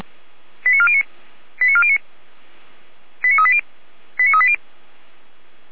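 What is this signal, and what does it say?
Five-tone selective call (ZVEI), heard as quick runs of five tones, tones of about 70 ms. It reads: 89409 89409 89380 89380
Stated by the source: ZVEI five-tone call 89356 received by SDR.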